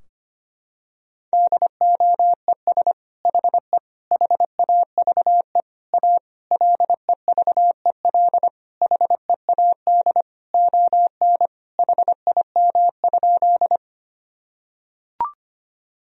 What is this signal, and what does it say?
Morse code, heard as a keyed tone, about 25 wpm, 700 Hz. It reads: DOES HE HAVE A LEVEL HEAD ON HIM?
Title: Sentences from Top 1000 Words - Review Part 4 - 25wpm